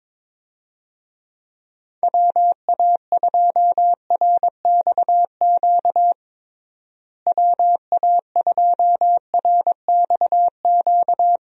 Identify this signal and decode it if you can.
Morse code, heard as a keyed tone, about 22 wpm, 700 Hz. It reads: WA2RXQ WA2RXQ
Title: USA Call Signs - 6 Characters - Repeated 2x - 22wpm